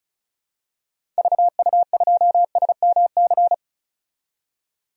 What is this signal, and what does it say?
Morse code, heard as a keyed tone, about 34 wpm, 700 Hz. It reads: VU2SMC
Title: DXCC-100 Call Signs - 6 Characters - 35wpm